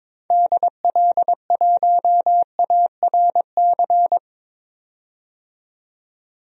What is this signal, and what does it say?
Morse code, heard as a keyed tone, about 22 wpm, 700 Hz. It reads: DL1ARC
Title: DXCC-100 Call Signs - 6 Characters - 22wpm